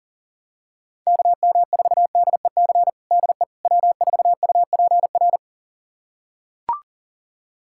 Morse code, heard as a keyed tone, about 40 wpm, 700 Hz. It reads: KM4DEC DE W4UPR
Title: USA Callsign DE Callsign 40wpm